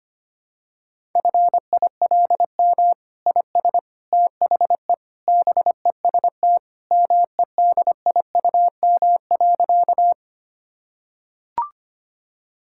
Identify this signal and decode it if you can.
Morse code, heard as a keyed tone, about 25 wpm, 700 Hz. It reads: FILM IS THE BEST MEDIUM.
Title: Sentences from Top 1000 Words - Review Part 6 - 25wpm